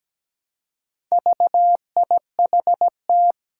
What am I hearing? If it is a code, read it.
Morse code, heard as a keyed tone, about 17 wpm, 700 Hz. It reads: VIHT
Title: Four Challenging Letter-Number Combinations 17wpm